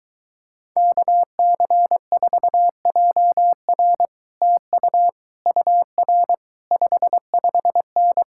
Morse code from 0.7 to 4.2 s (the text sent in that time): KC4JR